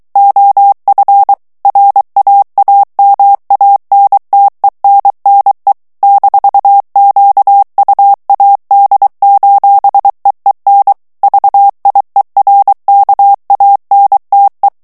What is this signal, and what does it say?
Morse code, heard as a keyed tone, about 23 wpm, 800 Hz. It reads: OF RAAMANTENNE -QUAD:EEN VIERKANTE
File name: HOLLANDAIS texte clair 1 vitesse 22 mots minute